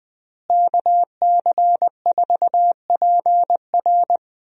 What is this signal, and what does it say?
Morse code, heard as a keyed tone, about 20 wpm, 700 Hz. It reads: KC4PR
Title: USA Call Signs - 5 Characters - Repeated 2x - 20wpm